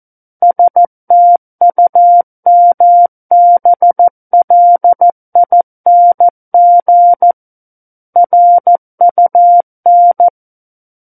Morse code, heard as a keyed tone, about 14 wpm, 700 Hz. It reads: STUMBLING RUN
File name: Book One / 05-Chapter 5 - 14 WPM